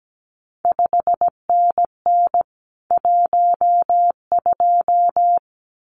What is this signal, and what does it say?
Morse code, heard as a keyed tone, about 17 wpm, 700 Hz. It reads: 5NN 12
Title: CQ WW DX Contest Exchange Encore - 17wpm